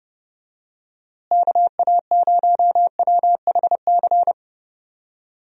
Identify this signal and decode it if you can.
Morse code, heard as a keyed tone, about 30 wpm, 700 Hz. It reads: KA0WHC
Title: USA Call Signs - 6 Characters - Repeated 2x - 30wpm